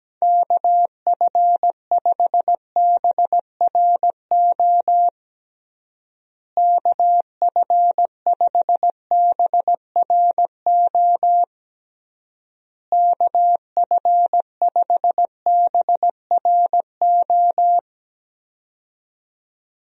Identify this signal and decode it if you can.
Morse code, heard as a keyed tone, about 17 wpm, 700 Hz. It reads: KF5BRO KF5BRO KF5BRO
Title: USA Call Signs - 6 Characters - Repeated 3x - 17wpm